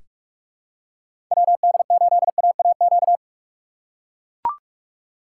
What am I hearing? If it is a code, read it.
Morse code, heard as a keyed tone, about 42 wpm, 700 Hz. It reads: WD8AAQ
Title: USA Call Signs - 6 Characters - 2x Character Spacing - 45wpm